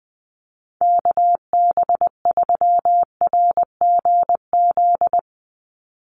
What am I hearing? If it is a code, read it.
Morse code, heard as a keyed tone, about 19 wpm, 700 Hz. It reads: KB3RGZ
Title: USA Call Signs 20wpm